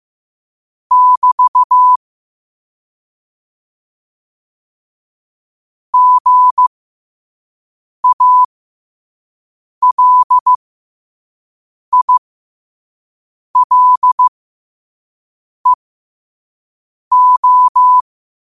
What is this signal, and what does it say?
Morse code, heard as a keyed tone, about 15 wpm, spaced as for 5 wpm, 1000 Hz. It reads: <BT> GALILEO
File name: qotdfiles / 5wpm